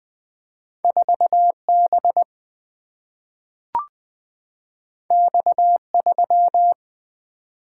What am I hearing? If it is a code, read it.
Morse code, heard as a keyed tone, about 20 wpm, 700 Hz. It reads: 4 B X 3
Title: Single-Letter Single-Number 20wpm